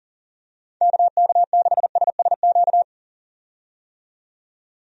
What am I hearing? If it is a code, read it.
Morse code, heard as a keyed tone, about 40 wpm, 700 Hz. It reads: KK6SSQ
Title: USA Call Signs - 6 Characters - 40wpm